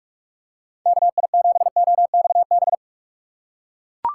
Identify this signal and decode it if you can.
Morse code, heard as a keyed tone, about 43 wpm, 690 Hz. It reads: KI7OXB
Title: USA Call Signs - 4x Character Spacing - 45wpm